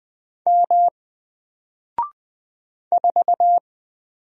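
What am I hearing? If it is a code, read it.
Morse code, heard as a keyed tone, about 20 wpm, 700 Hz. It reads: M 4
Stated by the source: Single letter and number Morse drill at 20 wpm, from N4LCD.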